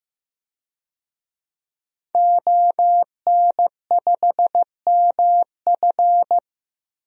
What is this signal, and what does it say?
Morse code, heard as a keyed tone, about 15 wpm, 700 Hz. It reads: ON5MF
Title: DXCC-100 Call Signs - 5 Characters - Repeated 3x - 15wpm